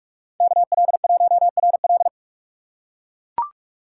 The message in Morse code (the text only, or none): KL1RL